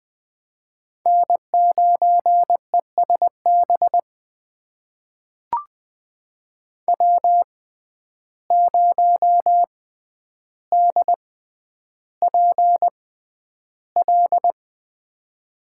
Morse code, heard as a keyed tone, about 20 wpm, spaced as for 10 wpm, 700 Hz. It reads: N9ESB W0DPL
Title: USA Call Signs - 5 Characters - 6x Character Spacing - 20wpm